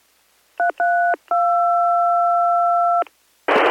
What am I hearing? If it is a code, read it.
DTMF, heard as telephone touch tones, tones of about 230 ms. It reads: 332